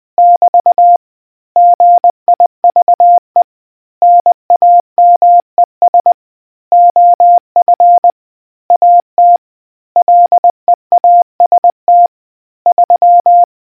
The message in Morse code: <BT> GIVE NAMES OF AT LEAST 3